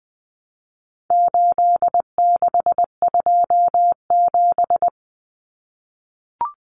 Morse code, heard as a keyed tone, about 20 wpm, 700 Hz. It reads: 8627
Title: Numbers - 4-Digit 20wpm